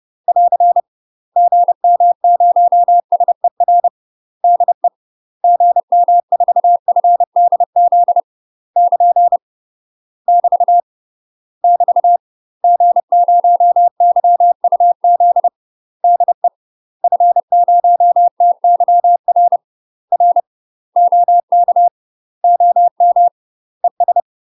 Morse code, heard as a keyed tone, about 30 wpm, 700 Hz. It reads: <AR> GM0SER DE GM4FDZ <KN> <BT> <BT> G0YUZ DE F0TYR R OK OM ES
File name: UK MASTER QSOs TRAINING / 30WPM / UK MASTER QSOs 0030